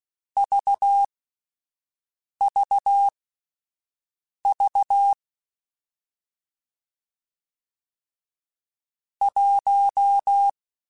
Morse code, heard as a keyed tone, about 16 wpm, spaced as for 5 wpm, 800 Hz. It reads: VVV 1